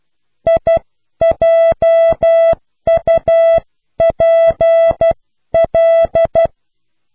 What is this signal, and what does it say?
Morse code, heard as a keyed tone, about 12 wpm, 660 Hz. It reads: IJUPL